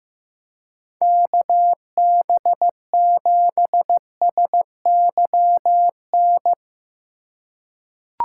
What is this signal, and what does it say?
Morse code, heard as a keyed tone, about 16 wpm, 700 Hz. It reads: KB7SYN
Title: USA Call Signs - 4x Character Spacing - 15wpm